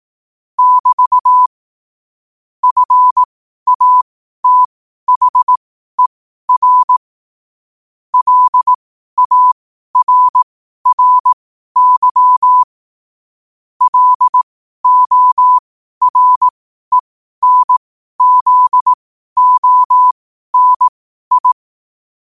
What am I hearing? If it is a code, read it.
Morse code, heard as a keyed tone, about 18 wpm, spaced as for 12 wpm, 1000 Hz. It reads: <BT> FATHER LARRY LORENZONI
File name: qotdfiles / 12wpm